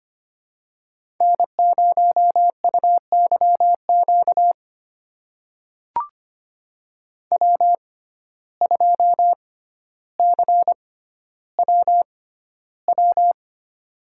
Morse code, heard as a keyed tone, about 25 wpm, spaced as for 13 wpm, 700 Hz. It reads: N0UYQ W2CWW
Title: USA Call Signs - 5 Characters - 6x Character Spacing - 25wpm